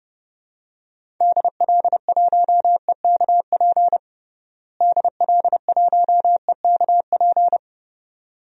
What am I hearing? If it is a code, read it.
Morse code, heard as a keyed tone, about 30 wpm, 700 Hz. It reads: DL1EKP DL1EKP
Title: DXCC-100 Call Signs - 6 Characters - Repeated 2x - 30wpm